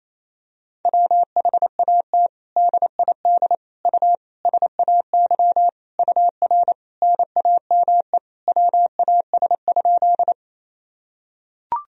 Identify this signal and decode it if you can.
Morse code, heard as a keyed tone, about 28 wpm, 700 Hz. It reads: WHAT DID U SAY UR NAME WAS?
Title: Sentences from Top 200 Words and Abbreviations 28wpm